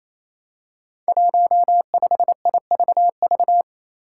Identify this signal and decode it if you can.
Morse code, heard as a keyed tone, about 28 wpm, 700 Hz. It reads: 15IVV
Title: Five Challenging Letter-Number Combinations 28wpm